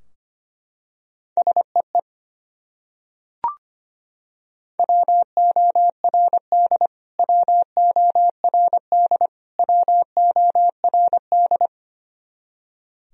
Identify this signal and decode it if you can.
Morse code, heard as a keyed tone, about 25 wpm, 700 Hz. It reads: SEE WORD WORD WORD
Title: Top 100 Words Repeated 25wpm